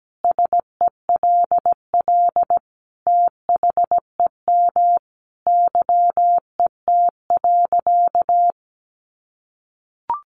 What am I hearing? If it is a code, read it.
Morse code, heard as a keyed tone, about 17 wpm, 700 Hz. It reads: SELL THEM YET.
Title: Sentences from Top 1100 Words 17wpm